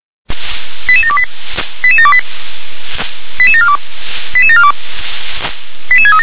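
Five-tone selective call (ZVEI), heard as quick runs of five tones, tones of about 70 ms. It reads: 88728 88728 88742 88742 88637
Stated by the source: Five-tone ZVEI selective call received on an SDR.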